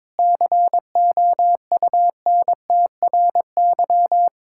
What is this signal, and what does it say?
Morse code, heard as a keyed tone, about 22 wpm, 700 Hz. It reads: COUNTRY
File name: practice / 22wpm / Sentences with 3-Word Phrases - 22wpm